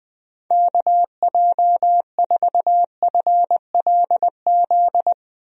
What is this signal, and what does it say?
Morse code, heard as a keyed tone, about 20 wpm, 700 Hz. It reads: KJ4FLZ
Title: USA Call Signs - 6 Characters - 20wpm